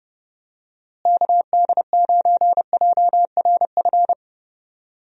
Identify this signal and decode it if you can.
Morse code, heard as a keyed tone, about 30 wpm, 700 Hz. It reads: KD9JRF
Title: USA Call Signs - 6 Characters - Repeated 3x - 30wpm